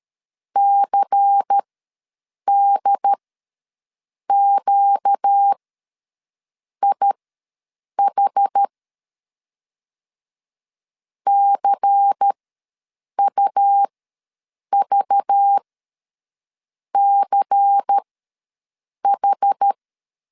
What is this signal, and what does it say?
Morse code, heard as a keyed tone, about 13 wpm, spaced as for 6 wpm, 800 Hz. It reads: CDQIH CUVCH
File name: LEZIONI / 5 LBVC / 5 LBVC ws 12wpm 3159